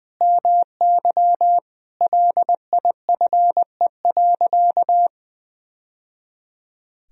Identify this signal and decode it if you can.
Morse code, heard as a keyed tone, about 20 wpm, 700 Hz. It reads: MY LIFE.
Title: Sentences from Top 1000 Words - Review Part 1 - 20wpm